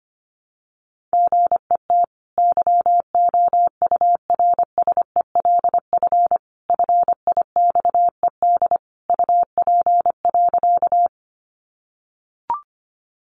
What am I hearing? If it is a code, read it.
Morse code, heard as a keyed tone, about 25 wpm, 700 Hz. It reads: GET YOURSELF FIXED UP.
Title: Sentences from Top 1100 Words 25wpm